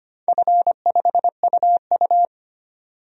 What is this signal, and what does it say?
Morse code, heard as a keyed tone, about 25 wpm, 690 Hz. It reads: F5UU
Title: DXCC-100 Call Signs - 4 Characters - 25wpm